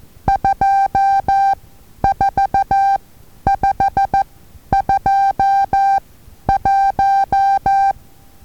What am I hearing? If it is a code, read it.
Morse code, heard as a keyed tone, about 14 wpm, spaced as for 10 wpm, 790 Hz. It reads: 24521
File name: Taviro HA6VA A